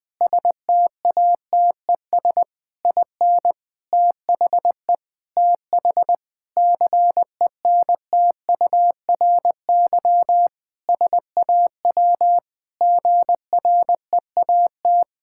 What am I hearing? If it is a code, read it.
Morse code, heard as a keyed tone, about 20 wpm, 700 Hz. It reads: STATES IN THE TH CENTURY SAW GREAT
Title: Sentences with 5-Word Phrases - Spoken Beginning - 20wpm